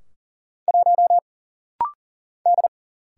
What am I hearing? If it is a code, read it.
Morse code, heard as a keyed tone, about 40 wpm, 700 Hz. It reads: 1 D
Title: Single Letter-Number - Rapid-Fire 40wpm